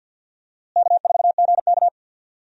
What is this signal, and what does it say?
Morse code, heard as a keyed tone, about 53 wpm, 700 Hz. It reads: K4GK
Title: USA Call Signs - 4 Characters - Repeated 3x - 50wpm